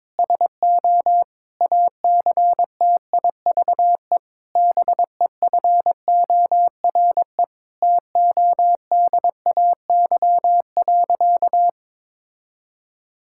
Morse code, heard as a keyed tone, about 22 wpm, 700 Hz. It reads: SO ACTIVE BEFORE TODAY.
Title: Sentences from Top 1000 Words - Review Part 6 - 22wpm